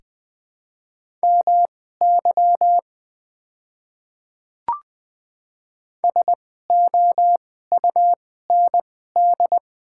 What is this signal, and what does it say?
Morse code, heard as a keyed tone, about 20 wpm, spaced as for 15 wpm, 690 Hz. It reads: MY SOUND
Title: 2x Character Spacing - Top 200 Words - 20wpm